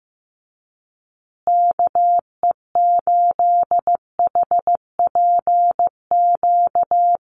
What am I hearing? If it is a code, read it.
Morse code, heard as a keyed tone, about 15 wpm, 700 Hz. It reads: KE8HPQ
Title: USA Call Signs Repeated 3x 15wpm